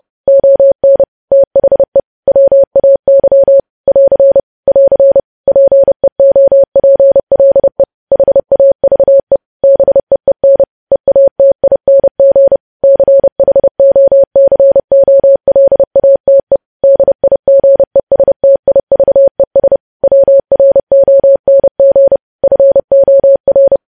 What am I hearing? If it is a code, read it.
Morse code, heard as a keyed tone, about 30 wpm, 550 Hz. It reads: ON THE WAY <AR> <AR> PEOPLE HAVE BEEN EATING CHOCOLATE DIGESTIVES WRONG FOR
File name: podcast / News headlines-1x-30x30